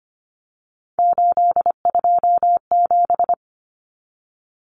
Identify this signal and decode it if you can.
Morse code, heard as a keyed tone, about 25 wpm, 700 Hz. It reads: 827
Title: Common Cut Numbers - 3-Digit 25wpm